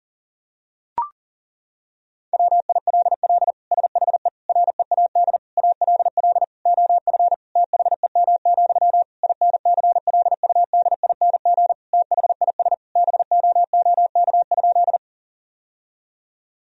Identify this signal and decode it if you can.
Morse code, heard as a keyed tone, about 40 wpm, 700 Hz. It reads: WILL SHE READ ALL OF THEM, INCLUDING THIS BOOK?
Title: Sentences from Top 1000 Words - Review Part 4 - 40wpm